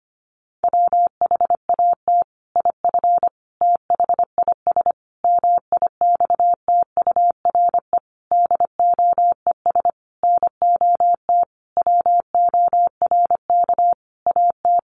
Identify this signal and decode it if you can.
Morse code, heard as a keyed tone, about 25 wpm, 700 Hz. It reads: WHAT IF THIS MIXTURE DOES NOT WORK AT
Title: Sentences from Top 1500 Words 25wpm